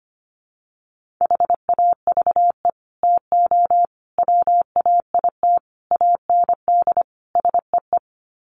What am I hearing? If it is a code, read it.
Morse code, heard as a keyed tone, about 25 wpm, 700 Hz. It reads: HAVE TO WAIT AND SEE
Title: Sets of 5 Words Encore2 25wpm